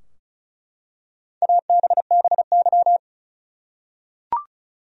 Morse code, heard as a keyed tone, about 34 wpm, 700 Hz. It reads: ABBY
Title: USA Common Names - 35wpm